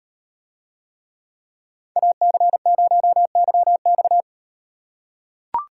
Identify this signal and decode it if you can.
Morse code, heard as a keyed tone, about 38 wpm, 700 Hz. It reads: AC0YX